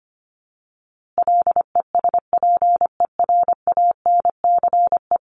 Morse code, heard as a keyed tone, about 25 wpm, 700 Hz. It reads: LESPERANCE
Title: A Sound Of Thunder 25wpm